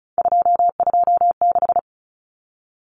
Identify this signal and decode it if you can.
Morse code, heard as a keyed tone, about 34 wpm, 700 Hz. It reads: 226
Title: Common Cut Numbers - 3-Digit 35wpm